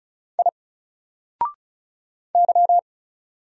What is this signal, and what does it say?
Morse code, heard as a keyed tone, about 35 wpm, 700 Hz. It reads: I Y
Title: Single Character (Letter:Number:Punctuation:Prosign) Alt 35wpm